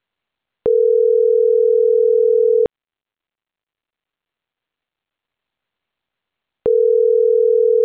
Ringback tone (North America), 440 + 480 Hz.